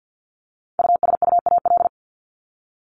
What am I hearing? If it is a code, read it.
Morse code, heard as a keyed tone, about 50 wpm, 700 Hz. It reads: USUAL
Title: Top 500 Words Repeated 50wpm